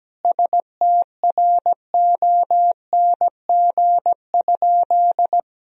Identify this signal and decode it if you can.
Morse code, heard as a keyed tone, about 17 wpm, 700 Hz. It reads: STRONG?
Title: Sentences from Top 1000 Words - Review Part 3 - 17wpm